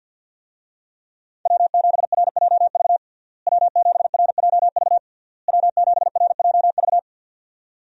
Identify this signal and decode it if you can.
Morse code, heard as a keyed tone, about 53 wpm, 700 Hz. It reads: W7RJV W7RJV W7RJV
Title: USA Call Signs - 5 Characters - Repeated 3x - 50wpm